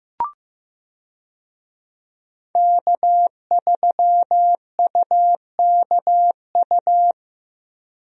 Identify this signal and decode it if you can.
Morse code, heard as a keyed tone, about 15 wpm, 700 Hz. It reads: K3UKU
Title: USA Call Signs - 5 Characters - Repeated 3x - 15wpm